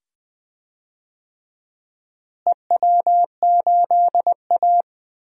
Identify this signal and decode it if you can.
Morse code, heard as a keyed tone, about 20 wpm, 700 Hz. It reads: EW8A
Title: DXCC-100 Call Signs - 4 Characters - Repeated 2x - 20wpm